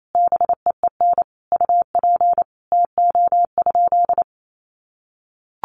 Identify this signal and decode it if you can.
Morse code, heard as a keyed tone, about 28 wpm, 700 Hz. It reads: BEEN UP TO?
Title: Sentences from Top 100 Words 28wpm